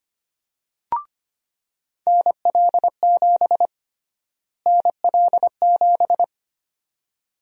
Morse code, heard as a keyed tone, about 25 wpm, 700 Hz. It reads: NL7 NL7
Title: USA Call Sign Prefixes - Repeated 2x - 25wpm